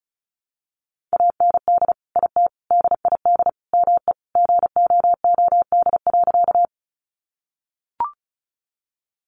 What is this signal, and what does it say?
Morse code, heard as a keyed tone, about 34 wpm, 700 Hz. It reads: AND IT DID ME GOOD.
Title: Sentences from Top 200 Words 35wpm